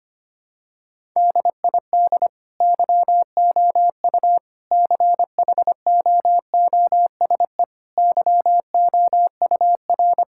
Morse code, heard as a keyed tone, about 25 wpm, 700 Hz. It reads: DID YOU CHOOSE YOUR
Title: Sentences from Top 1000 Words - Review Part 6 - 25wpm